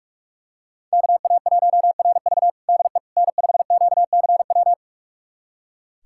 Morse code, heard as a keyed tone, about 41 wpm, 700 Hz. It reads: KA1RV DE N5QCW
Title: USA Callsign DE Callsign - 5 Characters - 45wpm